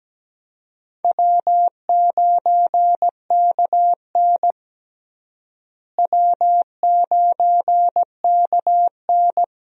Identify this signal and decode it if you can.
Morse code, heard as a keyed tone, about 17 wpm, 700 Hz. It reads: W9KN W9KN
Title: USA Call Signs - 4 Characters - Repeated 2x - 17wpm